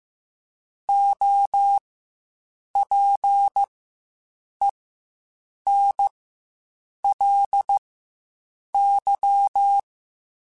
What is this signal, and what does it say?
Morse code, heard as a keyed tone, about 15 wpm, spaced as for 7 wpm, 800 Hz. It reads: OPENLY
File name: Quotes Spaced / Quotes Spaced-mp3s / Advice sp @10